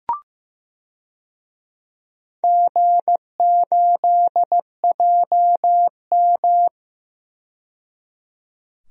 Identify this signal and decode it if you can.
Morse code, heard as a keyed tone, about 15 wpm, 700 Hz. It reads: G8JM
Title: DXCC-100 Call Signs - 4 Characters - 15wpm